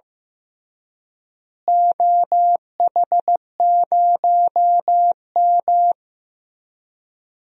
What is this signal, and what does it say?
Morse code, heard as a keyed tone, about 15 wpm, 700 Hz. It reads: OH0M
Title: DXCC-100 Call Signs - 4 Characters - Repeated 3x - 15wpm